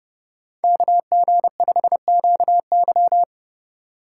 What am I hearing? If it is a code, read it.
Morse code, heard as a keyed tone, about 30 wpm, 700 Hz. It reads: KG5QY
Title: USA Call Signs - 5 Characters - 4x Character Spacing - 30wpm